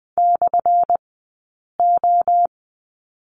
Morse code, heard as a keyed tone, about 20 wpm, spaced as for 8 wpm, 700 Hz. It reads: /O
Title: 177 Lesson 29d - Slash - Two Characters 20wpm